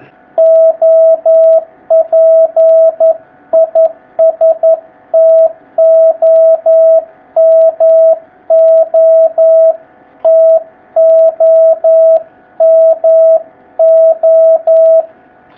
Morse code, heard as a keyed tone, about 11 wpm, 650 Hz. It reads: OP IS TOMO TOMO
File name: jh8cbh / 2019cw